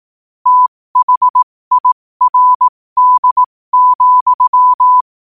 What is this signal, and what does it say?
Morse code, heard as a keyed tone, about 18 wpm, spaced as for 15 wpm, 1000 Hz. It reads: THIRD,
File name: qotdfiles / 15wpm